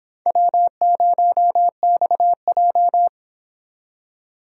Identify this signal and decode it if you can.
Morse code, heard as a keyed tone, about 29 wpm, 700 Hz. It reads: W0XJ